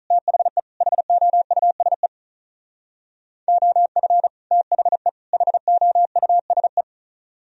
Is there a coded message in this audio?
Yes: Morse code from 0.1 to 6.8 s, heard as a keyed tone, about 46 wpm, 700 Hz. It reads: THE HOUSE OF THE HOUSE